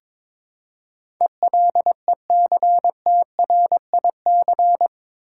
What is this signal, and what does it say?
Morse code, heard as a keyed tone, about 22 wpm, 700 Hz. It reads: ELECTRIC
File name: practice / 22wpm / Eight Letter Words - 22wpm